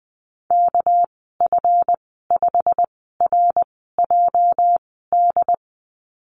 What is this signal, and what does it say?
Morse code, heard as a keyed tone, about 20 wpm, spaced as for 15 wpm, 700 Hz. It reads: KF5RJD
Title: 219 Lesson 35g - Number 0 - Call Signs 20wpm